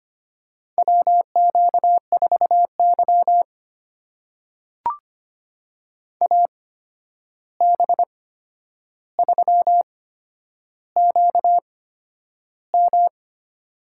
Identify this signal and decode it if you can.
Morse code, heard as a keyed tone, about 25 wpm, spaced as for 10 wpm, 700 Hz. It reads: WQ4Y AB3QM
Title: USA Call Signs - 8x Character Spacing - 25wpm